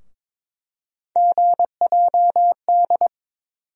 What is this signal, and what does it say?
Morse code, heard as a keyed tone, about 22 wpm, 700 Hz. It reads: GJD
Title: USA Call Sign Postfixes - 3 Letters 22wpm